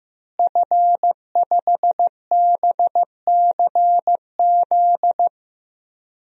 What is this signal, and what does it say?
Morse code, heard as a keyed tone, about 16 wpm, 700 Hz. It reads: F5BCZ
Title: DXCC-100 Call Signs - 5 Characters - Repeated 3x - 15wpm